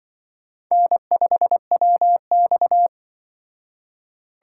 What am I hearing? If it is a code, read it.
Morse code, heard as a keyed tone, about 23 wpm, 700 Hz. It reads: N5WX
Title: Speed Racing - USA Call Signs 20wpm